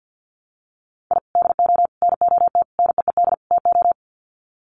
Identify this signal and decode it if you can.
Morse code, heard as a keyed tone, about 50 wpm, 700 Hz. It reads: I DO NOT NEED TO
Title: Sets of 5 Words 50wpm